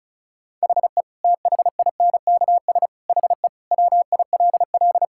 Morse code, heard as a keyed tone, about 36 wpm, 700 Hz. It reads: HE THINKS HE WILL
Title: Sentences from Top 1000 Words - Review Part 6 - 35wpm